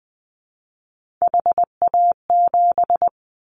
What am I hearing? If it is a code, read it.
Morse code, heard as a keyed tone, about 20 wpm, 700 Hz. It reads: HA7
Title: Callsign Prefixes for Top 25 DXCC - 20wpm